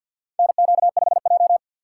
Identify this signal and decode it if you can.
Morse code, heard as a keyed tone, about 52 wpm, 700 Hz. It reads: NQ5J
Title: USA Call Signs - 4 Characters - 6x Character Spacing - 50wpm